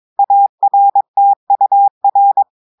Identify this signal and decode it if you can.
Morse code, heard as a keyed tone, about 22 wpm, 800 Hz. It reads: ARTUR
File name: audios / portugais / portugais texte en clair 1 vitesse 22 mots minutes